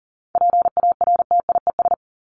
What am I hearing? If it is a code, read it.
Morse code, heard as a keyed tone, about 40 wpm, 700 Hz. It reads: PARTIES
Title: ES:S Suffix - Top 100 Words with Base Words 40wpm